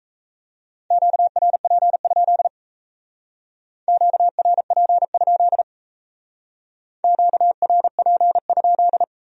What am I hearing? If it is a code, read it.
Morse code, heard as a keyed tone, about 35 wpm, 700 Hz. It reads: QRP? QRP? QRP?